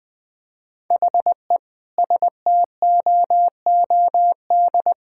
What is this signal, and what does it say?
Morse code, heard as a keyed tone, about 20 wpm, 700 Hz. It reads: HE STOOD
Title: Sentences with 5-Word Phrases - 20wpm